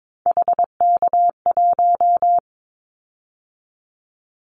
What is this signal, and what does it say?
Morse code, heard as a keyed tone, about 22 wpm, 700 Hz. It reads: HK1
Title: Callsign Prefixes for Top 100 DXCC - 22wpm